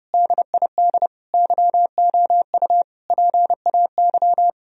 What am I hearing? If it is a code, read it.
Morse code, heard as a keyed tone, about 32 wpm, 700 Hz. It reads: DID YOU PAY
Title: Sentences from Top 1000 Words - Review Part 3 - 30wpm